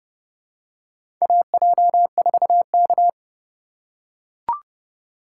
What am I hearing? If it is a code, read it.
Morse code, heard as a keyed tone, about 30 wpm, 700 Hz. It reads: AJ4K
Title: USA Call Signs - 4 Characters - 4x Character Spacing - 30wpm